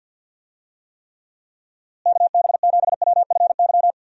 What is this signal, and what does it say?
Morse code, heard as a keyed tone, about 51 wpm, 700 Hz. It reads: KB7WFY
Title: USA Call Signs - 6 Characters - Repeated 2x - 50wpm